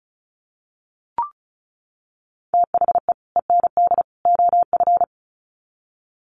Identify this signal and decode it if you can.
Morse code, heard as a keyed tone, about 34 wpm, 700 Hz. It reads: THE END OF
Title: Sets of 3 Words 35wpm